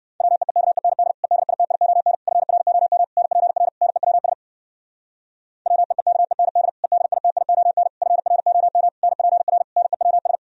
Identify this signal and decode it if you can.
Morse code, heard as a keyed tone, about 64 wpm, 700 Hz. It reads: WEEKEND EDITION FROM NPR NEWS WEEKEND EDITION FROM NPR NEWS